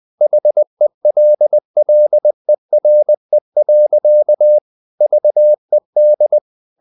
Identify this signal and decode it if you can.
Morse code, heard as a keyed tone, about 20 wpm, 600 Hz. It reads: HELLERE. VED
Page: Ferie 20 wpm